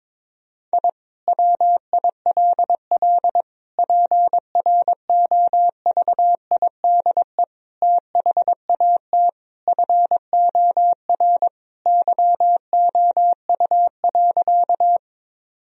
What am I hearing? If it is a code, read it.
Morse code, heard as a keyed tone, about 22 wpm, 700 Hz. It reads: I WILL PROVIDE THAT FOR YOU.
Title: Sentences from Top 1000 Words - Review Part 3 - 22wpm